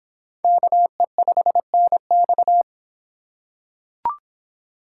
Morse code, heard as a keyed tone, about 24 wpm, 700 Hz. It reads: KE5NX